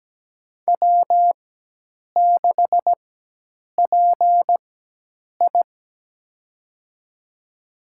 Morse code, heard as a keyed tone, about 17 wpm, spaced as for 8 wpm, 700 Hz. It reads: W6PI
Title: USA Call Signs - 4 Characters - 4x Character Spacing - 17wpm